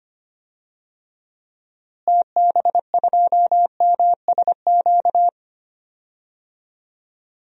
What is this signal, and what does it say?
Morse code, heard as a keyed tone, about 25 wpm, 700 Hz. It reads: TB2MSQ
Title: DXCC-100 Call Signs - 6 Characters - Repeated 3x - 25wpm